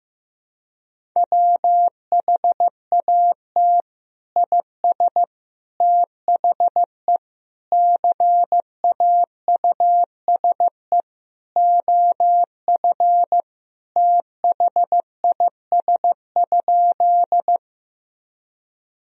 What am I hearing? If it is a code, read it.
Morse code, heard as a keyed tone, about 15 wpm, 700 Hz. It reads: WHAT IS THE CAUSE OF THIS?
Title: Sentences from Top 1000 Words - Review Part 1 - 15wpm